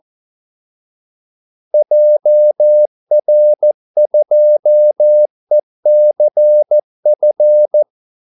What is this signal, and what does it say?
Morse code, heard as a keyed tone, about 14 wpm, 600 Hz. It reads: JR2ECF
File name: Callsigns-14wpm